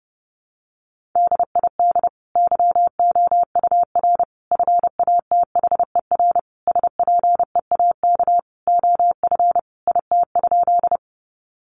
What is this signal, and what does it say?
Morse code, heard as a keyed tone, about 30 wpm, 700 Hz. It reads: DID YOUR FATHER SPEAK OF IT?
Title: Sentences from Top 1100 Words 30wpm